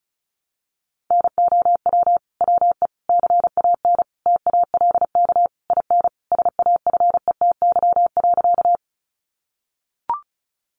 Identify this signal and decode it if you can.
Morse code, heard as a keyed tone, about 34 wpm, 700 Hz. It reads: NOW WE CAN TALK IN SAFETY.
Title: Sentences from Top 700 Words 35wpm